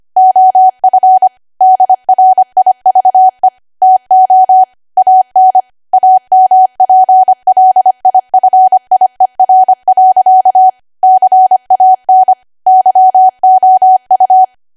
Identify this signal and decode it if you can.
Morse code, heard as a keyed tone, about 25 wpm, 750 Hz. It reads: OF DRIVE TO AN AMPLIFIER. CAN YOU
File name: Archive / 25 WPM / 25WPM